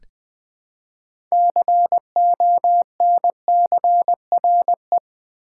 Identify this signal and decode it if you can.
Morse code, heard as a keyed tone, about 20 wpm, 700 Hz. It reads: CONCRE
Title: Build A Word - 3 Letters At A Time - 8-Letter Words - 20wpm